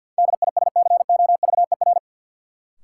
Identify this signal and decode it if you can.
Morse code, heard as a keyed tone, about 53 wpm, 700 Hz. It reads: DISCOVER
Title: Sentences from Top 1100 Words - New Word Drill Alt - 50wpm